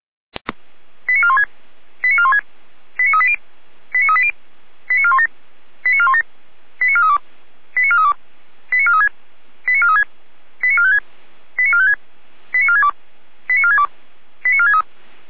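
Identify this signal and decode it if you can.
Five-tone selective call (ZVEI), heard as quick runs of five tones, tones of about 70 ms. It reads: 89416 89416 89380 89380 89417 89417 89432 89432 89436 89436 89456 89456 89462 89462 89463
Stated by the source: ZVEI five-tone selective call 89172, received by SDR.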